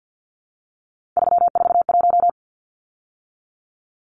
Morse code, heard as a keyed tone, about 50 wpm, 700 Hz. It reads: 341
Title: Numbers - 3-Digit 50wpm